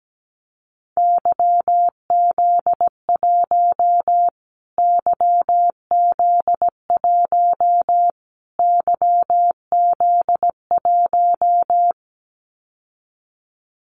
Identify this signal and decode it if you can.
Morse code, heard as a keyed tone, about 17 wpm, 700 Hz. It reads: YZ1 YZ1 YZ1
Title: Callsign Prefixes for Top 25 DXCC - Repeated 3x - 17wpm